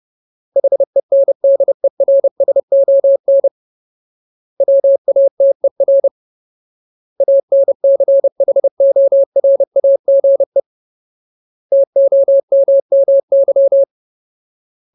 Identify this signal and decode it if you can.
Morse code, heard as a keyed tone, about 30 wpm, 550 Hz. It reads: HENDERSON WATER ANCHORAGE TOMMY
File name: Ultimate Word List / 30WPM Ultimate